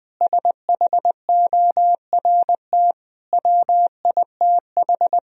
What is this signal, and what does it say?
Morse code, heard as a keyed tone, about 21 wpm, 700 Hz. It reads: SHORT WITH
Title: Sentences from Top 1000 Words - Review Part 2 - 20wpm